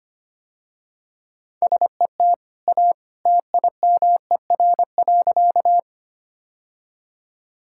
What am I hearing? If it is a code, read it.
Morse code, heard as a keyed tone, about 25 wpm, 700 Hz. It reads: SET A TIMER.
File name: Easy Sentences with Hints - 25wpm